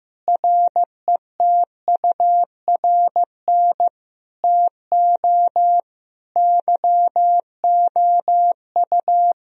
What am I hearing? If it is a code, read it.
Morse code, heard as a keyed tone, about 15 wpm, 700 Hz. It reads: RETURN TO YOU
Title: Sentences from Top 1000 Words - Review Part 4 - 15wpm